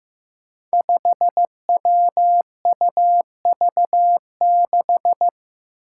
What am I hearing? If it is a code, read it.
Morse code, heard as a keyed tone, about 15 wpm, 700 Hz. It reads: 5WUV6
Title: Five Challenging Letter-Number Combinations 15wpm